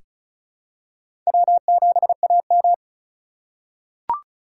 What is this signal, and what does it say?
Morse code, heard as a keyed tone, about 34 wpm, 700 Hz. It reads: W7AM
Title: USA Call Signs - 4 Characters - 4x Character Spacing - 35wpm